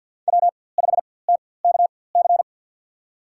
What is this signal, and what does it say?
Morse code, heard as a keyed tone, about 52 wpm, spaced as for 24 wpm, 700 Hz. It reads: W5TKC
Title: USA Call Signs - 5 Characters - 4x Character Spacing - 50wpm